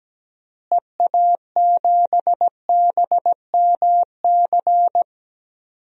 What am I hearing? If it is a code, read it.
Morse code, heard as a keyed tone, about 17 wpm, 700 Hz. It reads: EA7BMC
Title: DXCC-100 Call Signs - 6 Characters - Repeated 2x - 17wpm